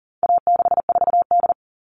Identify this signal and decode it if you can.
Morse code, heard as a keyed tone, about 40 wpm, 700 Hz. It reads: A64D
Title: All Cut Numbers - 4-Digit 40wpm